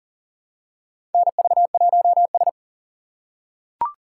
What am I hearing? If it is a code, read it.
Morse code, heard as a keyed tone, about 40 wpm, 700 Hz. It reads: NV1S
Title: USA Call Signs - 4 Characters - 4x Character Spacing - 40wpm